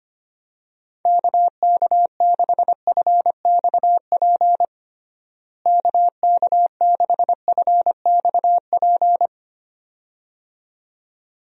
Morse code, heard as a keyed tone, about 25 wpm, 700 Hz. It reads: KK6FXP KK6FXP
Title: USA Call Signs - 6 Characters - Repeated 2x - 25wpm